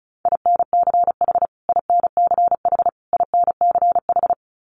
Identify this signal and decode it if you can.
Morse code, heard as a keyed tone, about 34 wpm, 700 Hz. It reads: INCH INCH INCH
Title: Top 500 Words Repeated 35wpm